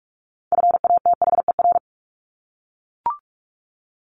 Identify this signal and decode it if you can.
Morse code, heard as a keyed tone, about 44 wpm, 700 Hz. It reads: FATHER
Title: Top 200 Words Repeated 45wpm